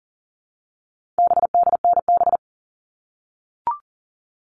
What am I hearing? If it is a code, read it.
Morse code, heard as a keyed tone, about 40 wpm, 700 Hz. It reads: BDNB
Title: All Cut Numbers - 4-Digit 40wpm